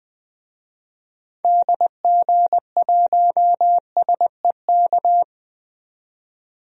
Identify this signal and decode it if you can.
Morse code, heard as a keyed tone, about 20 wpm, 700 Hz. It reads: DG1SEK
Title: DXCC-100 Call Signs - 6 Characters - 20wpm